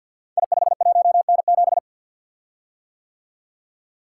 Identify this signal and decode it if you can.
Morse code, heard as a keyed tone, about 50 wpm, 700 Hz. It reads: I51N7
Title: Code Groups - 5 Letter:Numbers - 50wpm